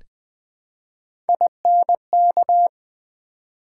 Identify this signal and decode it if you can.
Morse code, heard as a keyed tone, about 20 wpm, 700 Hz. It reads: INK
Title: Build A Word - 1 Letter At A Time - 3-Letter Words - 20wpm